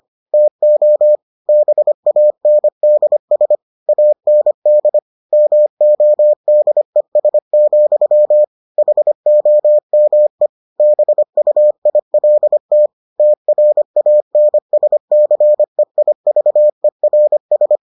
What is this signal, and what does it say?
Morse code, heard as a keyed tone, about 25 wpm, 600 Hz. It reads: TO BANDS AND MODES, HOME BUILT TRANSCEIVERS